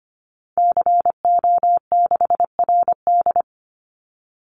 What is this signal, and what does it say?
Morse code, heard as a keyed tone, about 25 wpm, 700 Hz. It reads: CO6RD
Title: Call Signs Repeated 2x - 25wpm